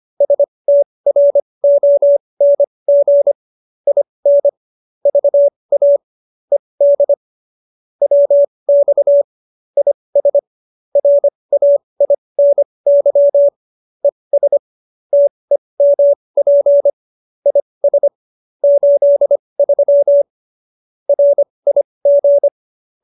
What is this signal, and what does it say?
Morse code, heard as a keyed tone, about 25 wpm, spaced as for 20 wpm, 570 Hz. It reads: STRONG IN VA ED WX IS RAINY ES TEMP IS 83 RIG
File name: QSO316 20